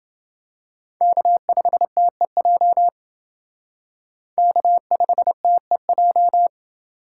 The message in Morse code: K5TEJ K5TEJ

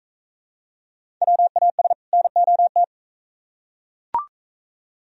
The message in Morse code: WAS NOT

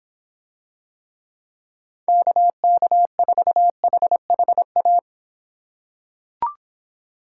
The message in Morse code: KK4HHA

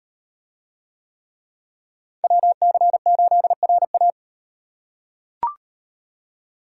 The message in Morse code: WC8RA

